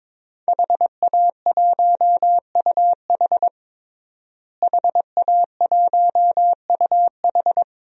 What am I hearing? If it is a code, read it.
Morse code, heard as a keyed tone, about 21 wpm, 700 Hz. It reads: HA1UH HA1UH
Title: DXCC-100 Call Signs - 5 Characters - Repeated 3x - 22wpm